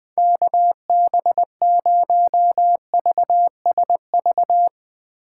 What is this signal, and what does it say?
Morse code, heard as a keyed tone, about 20 wpm, 700 Hz. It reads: KB0VSV
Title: Speed Racing - USA Call Signs 17wpm